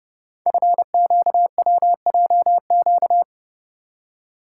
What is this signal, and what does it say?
Morse code, heard as a keyed tone, about 30 wpm, 700 Hz. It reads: FQWJQ
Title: Code Groups - 5 Letters - 30wpm